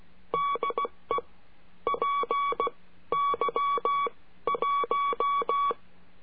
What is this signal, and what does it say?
Morse code, heard as a keyed tone, about 16 wpm, spaced as for 13 wpm, 1130 Hz. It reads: DE PY1
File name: cursos / CURSO DE TELEGRAFIA / Telegrafia - Trilha 22